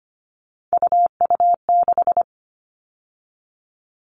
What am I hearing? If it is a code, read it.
Morse code, heard as a keyed tone, about 25 wpm, 700 Hz. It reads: UU6
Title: All Cut Numbers - 3-Digit 25wpm